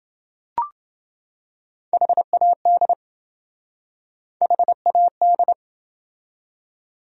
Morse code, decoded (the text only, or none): HAD HAD